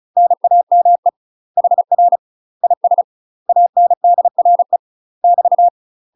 Morse code, heard as a keyed tone, about 33 wpm, 700 Hz. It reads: NAME HR IS ANDRE <BT>
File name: UK MASTER QSOs TRAINING/35WPM